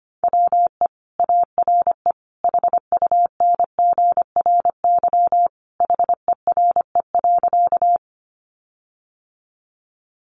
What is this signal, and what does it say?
Morse code, heard as a keyed tone, about 25 wpm, 700 Hz. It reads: WE ARE HUNGRY HERE.
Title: Sentences from Top 1400 Words 25wpm